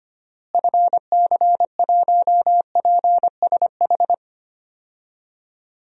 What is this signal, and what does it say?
Morse code, heard as a keyed tone, about 25 wpm, 700 Hz. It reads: FC1PSH
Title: DXCC-100 Call Signs - 6 Characters - Repeated 2x - 25wpm